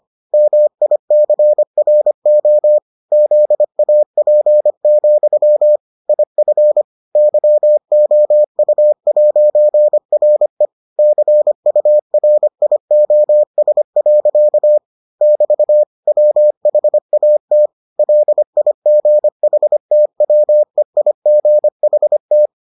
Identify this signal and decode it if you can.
Morse code, heard as a keyed tone, about 25 wpm, 600 Hz. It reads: MICRO ZAP, IF YOU'RE CURIOS. <BT> WHAT LIGHTWEIGHT